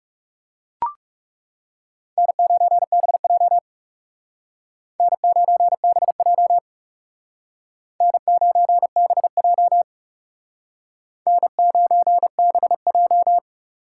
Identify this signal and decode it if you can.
Morse code, heard as a keyed tone, about 39 wpm, 700 Hz. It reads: N9BJ N9BJ N9BJ N9BJ